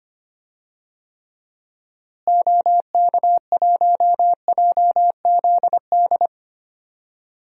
Morse code, heard as a keyed tone, about 25 wpm, 700 Hz. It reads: OK1JZD
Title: DXCC-100 Call Signs - 6 Characters - Repeated 2x - 25wpm